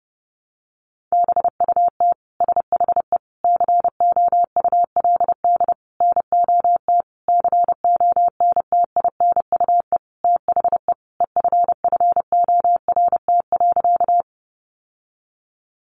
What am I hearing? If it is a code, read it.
Morse code, heard as a keyed tone, about 30 wpm, 700 Hz. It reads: BUT SHE COULD NOT CONTINUE THE EFFORT.
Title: Sentences from Top 1200 Words 30wpm